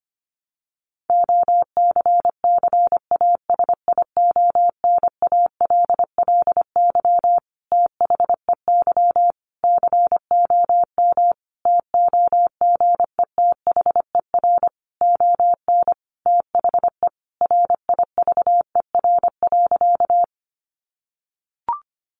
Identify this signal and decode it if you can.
Morse code, heard as a keyed tone, about 25 wpm, 700 Hz. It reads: OCCASIONALLY THEY COM TOGETHER ON THE RIVER.
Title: Sentences from Top 1300 Words 25wpm